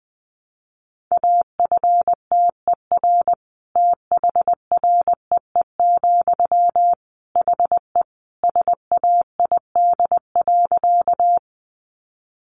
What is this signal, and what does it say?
Morse code, heard as a keyed tone, about 20 wpm, 700 Hz. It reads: AFTER THREE, HE SAID.
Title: Sentences from Top 200 Words 20wpm